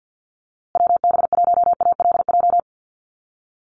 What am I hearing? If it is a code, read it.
Morse code, heard as a keyed tone, about 50 wpm, 700 Hz. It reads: WB1ALJ
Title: USA Call Signs Repeated 2x 50wpm